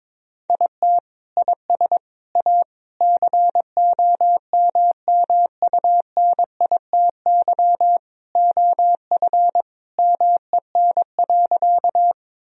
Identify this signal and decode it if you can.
Morse code, heard as a keyed tone, about 22 wpm, 700 Hz. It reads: IT IS A COMMUNITY OF MEN.
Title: Sentences from Top 1000 Words - Review Part 4 - 22wpm